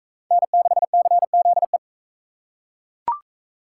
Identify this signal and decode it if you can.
Morse code, heard as a keyed tone, about 38 wpm, 700 Hz. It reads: N6CZE